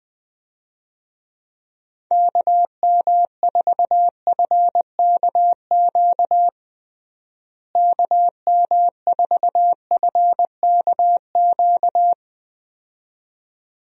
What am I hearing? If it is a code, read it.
Morse code, heard as a keyed tone, about 20 wpm, 700 Hz. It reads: KM4FKQ KM4FKQ
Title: USA Call Signs - 6 Characters - Repeated 2x - 20wpm